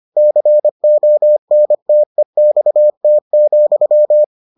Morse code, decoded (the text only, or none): CONTEXT,